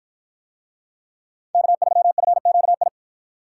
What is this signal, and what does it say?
Morse code, heard as a keyed tone, about 53 wpm, 700 Hz. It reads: K3FQI